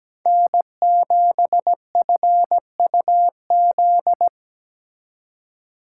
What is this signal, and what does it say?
Morse code, heard as a keyed tone, about 17 wpm, 700 Hz. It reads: N7FUZ
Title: Speed Racing - USA Call Signs 17wpm